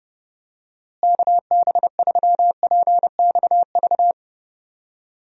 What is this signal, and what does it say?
Morse code, heard as a keyed tone, about 30 wpm, 700 Hz. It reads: KB3PXV
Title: USA Call Signs - 6 Characters - Repeated 3x - 30wpm